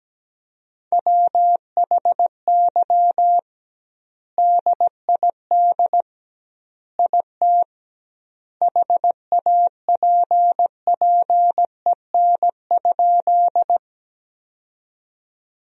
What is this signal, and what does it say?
Morse code, heard as a keyed tone, about 17 wpm, 700 Hz. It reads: WHY DID IT HAPPEN?
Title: 2x Word-Spacing - Sentences from Top 400 Words - 17wpm